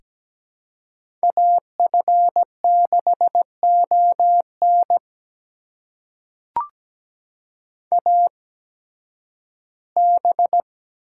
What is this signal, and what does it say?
Morse code, heard as a keyed tone, about 17 wpm, spaced as for 11 wpm, 700 Hz. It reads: AF6ON AB
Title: USA Call Signs - 5 Characters - 8x Character Spacing - 17wpm